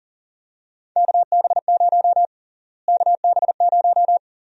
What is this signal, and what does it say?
Morse code, heard as a keyed tone, about 37 wpm, 700 Hz. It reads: KB0 KB0
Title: USA Call Sign Prefixes - Repeated 2x - 40wpm